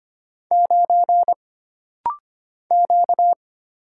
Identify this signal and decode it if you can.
Morse code, heard as a keyed tone, about 25 wpm, 700 Hz. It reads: 9 Q
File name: Single Character (Letter:Number:Punctuation:Prosign) - Rapid-Fire - 25wpm